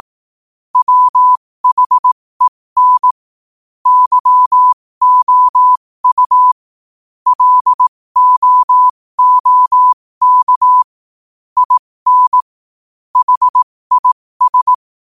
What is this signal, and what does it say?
Morse code, heard as a keyed tone, about 18 wpm, spaced as for 15 wpm, 1000 Hz. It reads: WHEN YOU LOOK IN HIS